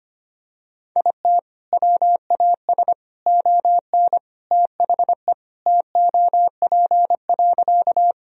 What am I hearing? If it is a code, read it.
Morse code, heard as a keyed tone, about 25 wpm, 700 Hz. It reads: IT WAS ON THE TOP.
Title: Sentences from Top 1000 Words - Review Part 2 - 25wpm